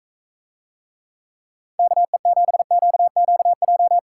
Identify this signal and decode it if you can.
Morse code, heard as a keyed tone, about 45 wpm, 700 Hz. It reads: KE7QQJ